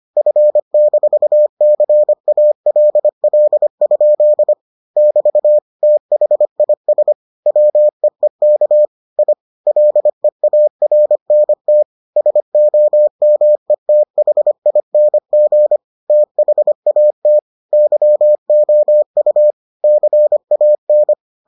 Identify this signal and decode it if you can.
Morse code, heard as a keyed tone, about 25 wpm, 600 Hz. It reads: F-CALL? <BT> THIS WEEK I LEARNT SOMETHING THAT YOU CAN